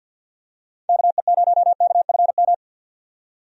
Morse code, heard as a keyed tone, about 46 wpm, 700 Hz. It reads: KE0KFM
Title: USA Call Signs - 6 Characters - 50wpm